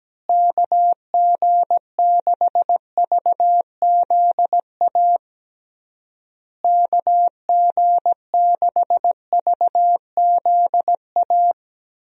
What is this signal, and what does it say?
Morse code, heard as a keyed tone, about 17 wpm, 700 Hz. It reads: KG6VZA KG6VZA
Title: USA Call Signs - 6 Characters - Repeated 3x - 17wpm